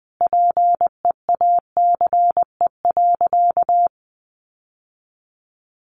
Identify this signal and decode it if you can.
Morse code, heard as a keyed tone, about 20 wpm, 700 Hz. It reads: PEACE.
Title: Sentences from Top 1500 Words 20wpm